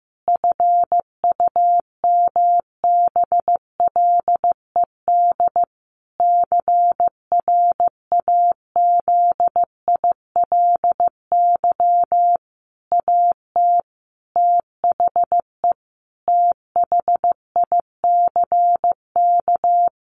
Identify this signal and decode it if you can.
Morse code, heard as a keyed tone, about 15 wpm, 700 Hz. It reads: FUMBLED CRAZILY AT THE THICK
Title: A Sound Of Thunder 15wpm